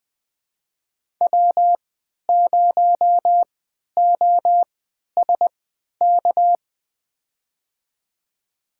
Morse code, heard as a keyed tone, about 20 wpm, spaced as for 11 wpm, 700 Hz. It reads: W0OSK
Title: USA Call Signs - 5 Characters - 3x Character Spacing - 20wpm